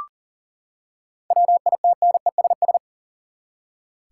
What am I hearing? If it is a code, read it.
Morse code, heard as a keyed tone, about 40 wpm, 700 Hz. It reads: WITNESS
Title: Seven Letter Words - 40wpm